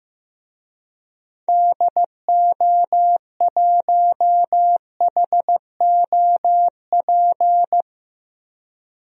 Morse code, heard as a keyed tone, about 16 wpm, 700 Hz. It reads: DO1HOP
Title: DXCC-100 Call Signs - 6 Characters - 15wpm